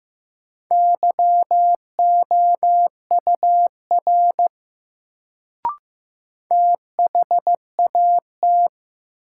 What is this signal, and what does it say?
Morse code, heard as a keyed tone, about 15 wpm, 700 Hz. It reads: YOUR THAT
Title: Sets of 2 Words - New Word Drill - 15wpm